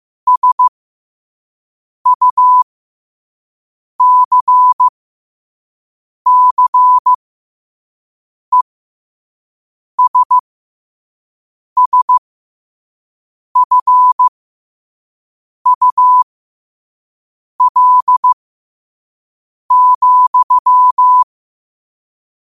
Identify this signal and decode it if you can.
Morse code, heard as a keyed tone, about 15 wpm, spaced as for 5 wpm, 1000 Hz. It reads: SUCCESSFUL,
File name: qotdfiles / 5wpm